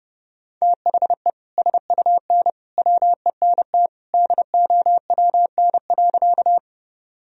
Morse code, heard as a keyed tone, about 32 wpm, 700 Hz. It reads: THE SUN WENT DOWN.
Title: Sentences from Top 1000 Words - Review Part 1 - 30wpm